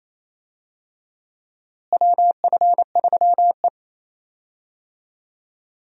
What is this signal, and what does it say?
Morse code, heard as a keyed tone, about 28 wpm, 700 Hz. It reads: WF3E
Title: USA Call Signs - 4 Characters - Repeated 2x - 28wpm